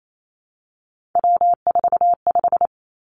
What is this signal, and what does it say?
Morse code, heard as a keyed tone, about 28 wpm, 700 Hz. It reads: W45
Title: Three Challenging Letter-Number Combinations 28wpm